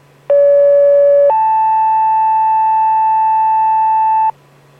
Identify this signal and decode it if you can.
Two-tone page, heard as a one-second tone, then a three-second tone, about 570 then 880 Hz, with noise. It is heard from 0.3 to 4.3 s.